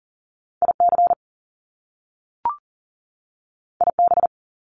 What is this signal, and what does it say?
Morse code, heard as a keyed tone, about 39 wpm, 700 Hz. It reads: IC IB
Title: Top Two Letter Combinations in Words 40wpm